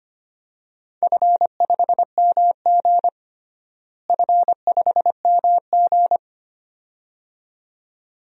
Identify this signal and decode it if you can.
Morse code, heard as a keyed tone, about 25 wpm, 700 Hz. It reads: F5MG F5MG
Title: DXCC-100 Call Signs - 4 Characters - Repeated 2x - 25wpm